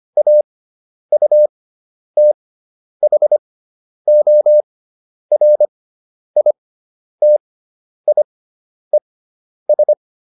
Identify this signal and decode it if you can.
Morse code, heard as a keyed tone, about 25 wpm, spaced as for 10 wpm, 600 Hz. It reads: AUTHORITIES
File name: SS213 10